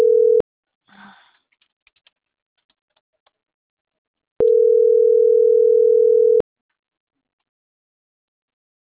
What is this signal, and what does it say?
Ringback tone (North America), 440 + 480 Hz.